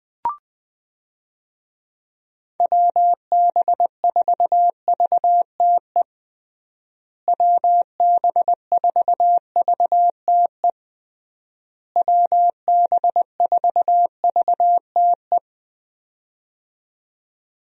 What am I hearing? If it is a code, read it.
Morse code, heard as a keyed tone, about 20 wpm, 700 Hz. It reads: WB4VTE WB4VTE WB4VTE
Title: USA Call Signs - 6 Characters - Repeated 3x - 20wpm